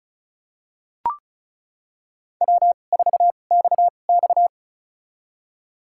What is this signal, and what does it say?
Morse code, heard as a keyed tone, about 36 wpm, spaced as for 25 wpm, 700 Hz. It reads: W4XX
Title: USA Call Signs - 4 Characters - 2x Character Spacing - 35wpm